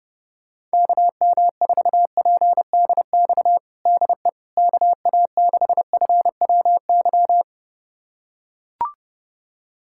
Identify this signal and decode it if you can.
Morse code, heard as a keyed tone, about 29 wpm, 700 Hz. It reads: KM4PDX DE KA6FWY
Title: USA Callsign DE Callsign - 6 Characters - 30wpm